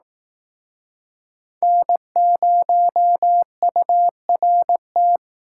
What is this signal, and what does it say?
Morse code, heard as a keyed tone, about 18 wpm, 700 Hz. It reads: N0URT
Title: Speed Racing - USA Call Signs 15wpm